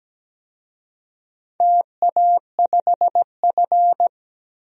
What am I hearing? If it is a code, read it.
Morse code, heard as a keyed tone, about 17 wpm, 700 Hz. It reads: TA5F
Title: DXCC-100 Call Signs - 4 Characters - 17wpm